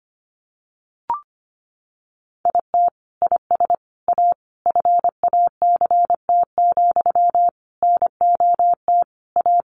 Morse code, heard as a keyed tone, about 25 wpm, 700 Hz. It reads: IT IS A FACT, NOT A
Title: Sentences from Top 1400 Words 25wpm